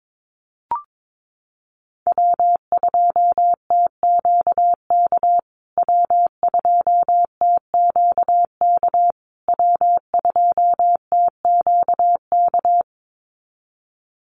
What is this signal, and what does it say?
Morse code, heard as a keyed tone, about 22 wpm, 700 Hz. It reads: W2TQK W2TQK W2TQK
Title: USA Call Signs Repeated 3x 22wpm